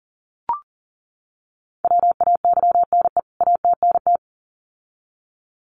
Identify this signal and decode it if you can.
Morse code, heard as a keyed tone, about 40 wpm, 700 Hz. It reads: WAYNE ATNT
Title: CWT Exchange 40wpm